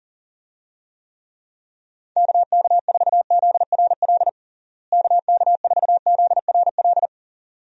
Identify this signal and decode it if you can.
Morse code, heard as a keyed tone, about 40 wpm, 700 Hz. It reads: KK4ZRL KK4ZRL
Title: USA Call Signs - 6 Characters - Repeated 2x - 40wpm